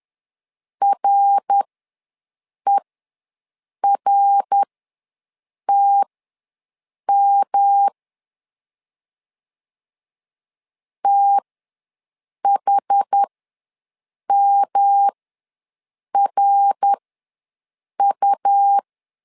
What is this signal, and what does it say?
Morse code, heard as a keyed tone, about 11 wpm, spaced as for 6 wpm, 800 Hz. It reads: RERTM THMRU
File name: LEZIONI / 3 MWRO / 3 MWR ws 10wpm 2151